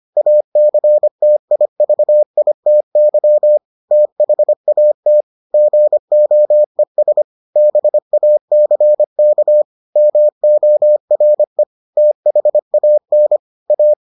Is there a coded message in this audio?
Yes: Morse code from 0.2 to 13.9 s, heard as a keyed tone, about 25 wpm, 600 Hz. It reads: ACTIVITY THAT GOES BACK MORE THAN A